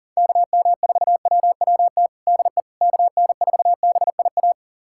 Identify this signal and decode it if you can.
Morse code, heard as a keyed tone, about 40 wpm, 700 Hz. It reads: KM4WWT DE KN4BIA
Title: USA Callsign DE Callsign 40wpm